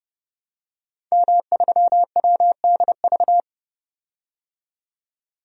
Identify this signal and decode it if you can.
Morse code, heard as a keyed tone, about 30 wpm, 700 Hz. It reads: M3WDV
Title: DXCC-100 Call Signs - 5 Characters - 30wpm